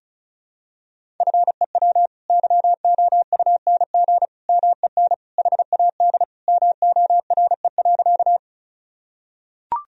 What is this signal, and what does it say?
Morse code, heard as a keyed tone, about 36 wpm, 700 Hz. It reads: FEW YOUNG MEN HAD MORE.
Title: Sentences from Top 1000 Words - Review Part 1 - 35wpm